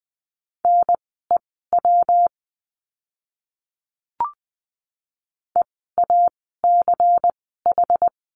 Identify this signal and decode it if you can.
Morse code, heard as a keyed tone, about 20 wpm, spaced as for 14 wpm, 700 Hz. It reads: NEW EACH
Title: 206 Lesson 33f - Letter j - Words 20wpm